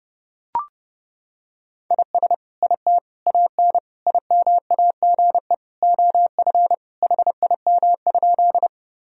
Morse code, heard as a keyed tone, about 30 wpm, 700 Hz. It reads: IS IT AN IMAGE OF HIM?
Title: Sentences from Top 1000 Words - Review Part 4 - 30wpm